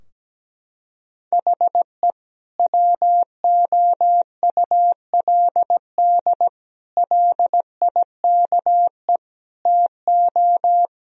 Morse code, heard as a keyed tone, about 17 wpm, 700 Hz. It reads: HE WOULD LIKE TO
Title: Sentences from Top 1000 Words - Review Part 3 - 17wpm